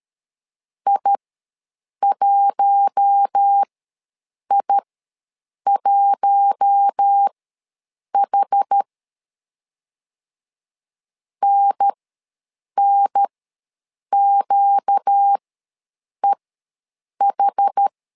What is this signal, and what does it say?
Morse code, heard as a keyed tone, about 13 wpm, spaced as for 7 wpm, 800 Hz. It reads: I1I1H NNQEH
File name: LEZIONI / 4 Q1 ws 12wpm 2129